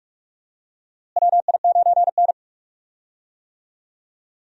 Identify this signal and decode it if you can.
Morse code, heard as a keyed tone, about 46 wpm, 700 Hz. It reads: WI9N
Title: USA Call Signs - 4 Characters - Repeated 3x - 45wpm